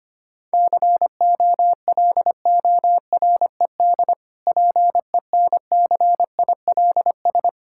Morse code, heard as a keyed tone, about 25 wpm, 700 Hz. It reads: COLORED PENCILS
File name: Easy Words - Speed-Racing with Hints - 25wpm